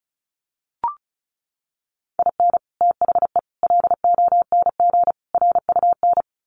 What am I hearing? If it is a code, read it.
Morse code, heard as a keyed tone, about 34 wpm, 700 Hz. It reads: IN THE LONG RUN
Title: Sets of 4 Words 35wpm